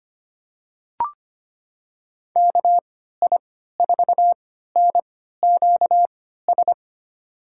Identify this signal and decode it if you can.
Morse code, heard as a keyed tone, about 25 wpm, spaced as for 14 wpm, 700 Hz. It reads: KI4NQS
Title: USA Call Signs - 3x Character Spacing - 25wpm